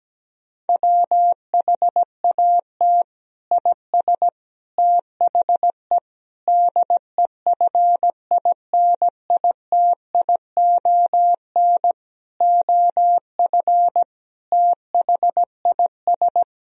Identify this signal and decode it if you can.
Morse code, heard as a keyed tone, about 17 wpm, 700 Hz. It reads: WHAT IS THE DEFINITION OF THIS
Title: Sentences from Top 1000 Words - Review Part 4 - 17wpm